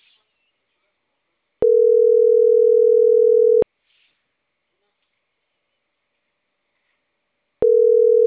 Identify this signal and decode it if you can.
Ringback tone (North America), 440 + 480 Hz.
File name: CENTRAL CAEA / out agcaea10